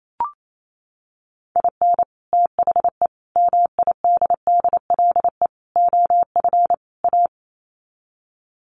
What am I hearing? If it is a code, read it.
Morse code, heard as a keyed tone, about 28 wpm, 700 Hz. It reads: IN THE MIDDLE OF A
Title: Sets of 5 Words 28wpm